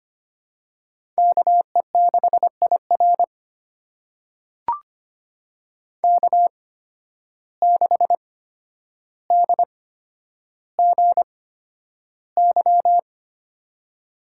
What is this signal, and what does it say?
Morse code, heard as a keyed tone, about 25 wpm, spaced as for 11 wpm, 700 Hz. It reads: KE6IR K6DGY
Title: USA Call Signs - 5 Characters - 8x Character Spacing - 25wpm